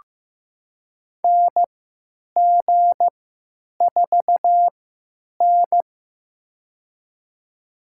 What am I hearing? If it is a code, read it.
Morse code, heard as a keyed tone, about 15 wpm, spaced as for 9 wpm, 700 Hz. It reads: NG4N
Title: USA Call Signs - 4 Characters - 3x Character Spacing - 15wpm